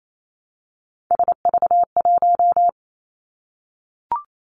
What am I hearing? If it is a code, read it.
Morse code, heard as a keyed tone, about 28 wpm, 700 Hz. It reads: SV1
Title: Callsign Prefixes for Top 50 DXCC - 28wpm